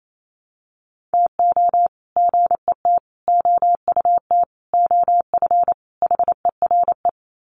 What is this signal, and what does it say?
Morse code, heard as a keyed tone, about 28 wpm, 700 Hz. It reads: TO GET OUT OF HERE
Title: Sets of 5 Words Encore 28wpm